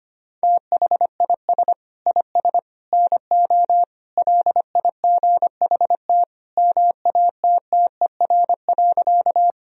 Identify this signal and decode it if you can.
Morse code, heard as a keyed tone, about 25 wpm, 700 Hz. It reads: THIS IS NO LIGHT MATTER.
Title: Sentences from Top 1000 Words - Review Part 1 - 25wpm